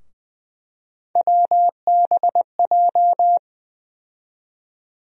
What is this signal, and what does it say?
Morse code, heard as a keyed tone, about 20 wpm, 700 Hz. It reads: WBJ
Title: USA Call Sign Postfixes - 3 Letters 20wpm